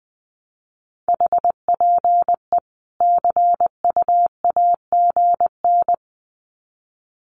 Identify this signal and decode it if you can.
Morse code, heard as a keyed tone, about 20 wpm, 700 Hz. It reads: HPE CUAGN
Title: Common QSO Phrases 20wpm